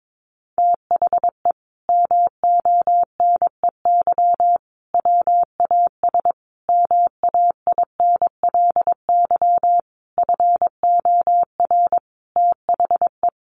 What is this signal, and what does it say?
Morse code, heard as a keyed tone, about 22 wpm, 700 Hz. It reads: THE MONEY WAS MAINLY FOR THE
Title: Sentences from Top 1400 Words 22wpm